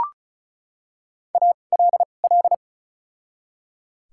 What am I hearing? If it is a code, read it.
Morse code, heard as a keyed tone, about 36 wpm, spaced as for 25 wpm, 700 Hz. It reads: ALL
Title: 2x Character Spacing - Top 100 Words - 35wpm